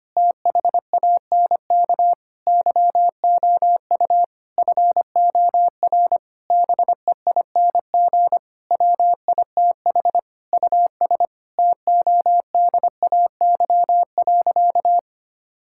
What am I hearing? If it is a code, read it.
Morse code, heard as a keyed tone, about 25 wpm, 700 Hz. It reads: THANK YOU FOR BEING WITH US TODAY.
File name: practice / 25wpm / Sentences with 9-Word Phrases - 25wpm